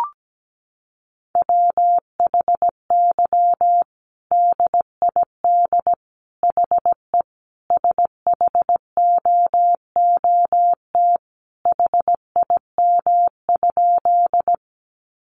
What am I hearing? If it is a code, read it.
Morse code, heard as a keyed tone, about 17 wpm, 700 Hz. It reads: WHY DID HE SHOOT HIM?
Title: Sentences from Top 1200 Words 17wpm